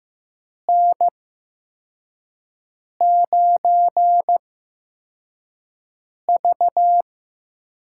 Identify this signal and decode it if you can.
Morse code, heard as a keyed tone, about 15 wpm, spaced as for 4 wpm, 700 Hz. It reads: N9V
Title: USA Call Signs - 4 Characters - 8x Character Spacing - 15wpm